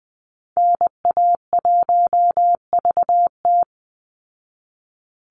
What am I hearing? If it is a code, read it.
Morse code, heard as a keyed tone, about 20 wpm, 700 Hz. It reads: NA1VT
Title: Call Signs Repeated 2x - 20wpm